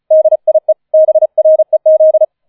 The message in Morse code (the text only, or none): DIE BREZ